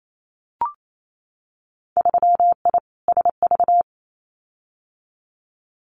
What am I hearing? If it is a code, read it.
Morse code, heard as a keyed tone, about 28 wpm, 700 Hz. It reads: 3I SV
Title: Winter Field Day Exchanges Encore 28wpm